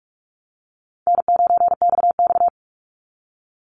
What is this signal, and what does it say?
Morse code, heard as a keyed tone, about 44 wpm, 700 Hz. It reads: N9XX
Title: Easy Call Signs Repeated 3x - 45wpm